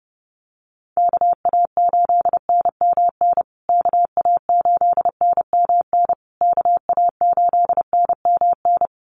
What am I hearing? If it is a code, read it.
Morse code, heard as a keyed tone, about 30 wpm, 700 Hz. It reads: KA8NMN KA8NMN KA8NMN
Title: USA Call Signs Repeated 3x 30wpm